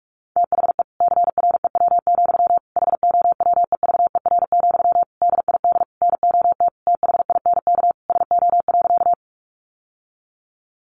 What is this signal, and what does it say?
Morse code, heard as a keyed tone, about 44 wpm, 700 Hz. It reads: THE CREW, HOWEVER, DID NOT THINK SO.
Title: Sentences from Top 1500 Words 45wpm